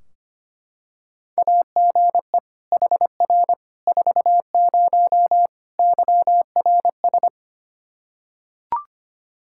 Morse code, heard as a keyed tone, about 25 wpm, 700 Hz. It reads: AGE HR 40 YRS
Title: Practice QSO - Exchange 3 - More Info - 25wpm